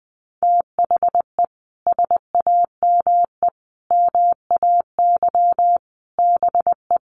Morse code, heard as a keyed tone, about 20 wpm, 700 Hz. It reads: THE SAME MAY BE
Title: Sentences from Top 1100 Words 20wpm